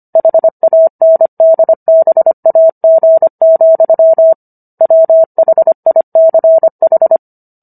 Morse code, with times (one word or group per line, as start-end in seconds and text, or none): 0.2-4.3 HANDBAG,
4.8-7.2 WHICH